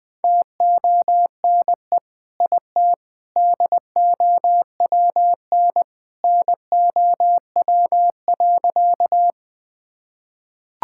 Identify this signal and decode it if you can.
Morse code, heard as a keyed tone, about 20 wpm, 700 Hz. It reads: TONE IT DOWN NOW.
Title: Sentences from Top 1600 Words 20wpm